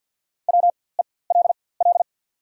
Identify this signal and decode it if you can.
Morse code, heard as a keyed tone, about 53 wpm, spaced as for 24 wpm, 700 Hz. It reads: WELL
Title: 4x Character Spacing - Top 300 Words - 50wpm